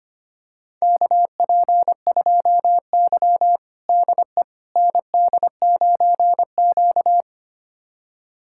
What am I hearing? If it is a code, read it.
Morse code, heard as a keyed tone, about 25 wpm, 700 Hz. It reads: KP2Y DE ND9Q
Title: USA Callsign DE Callsign - 4 Characters - 25wpm